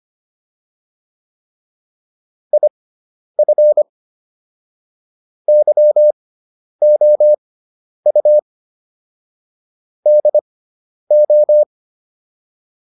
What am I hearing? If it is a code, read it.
Morse code, heard as a keyed tone, about 25 wpm, spaced as for 10 wpm, 600 Hz. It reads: IF YOU DO